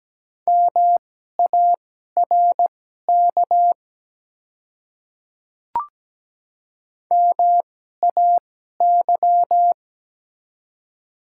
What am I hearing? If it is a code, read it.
Morse code, heard as a keyed tone, about 17 wpm, spaced as for 12 wpm, 700 Hz. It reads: MARK MAY
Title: 2x Character Spacing - Top 300 Words - 17wpm